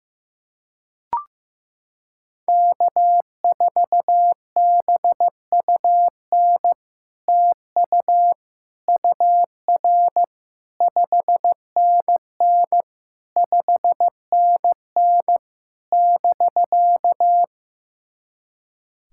Morse code, heard as a keyed tone, about 15 wpm, 700 Hz. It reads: K4BUN TU UR 5NN 5NN <BK>
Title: SOTA QSO Practice 15wpm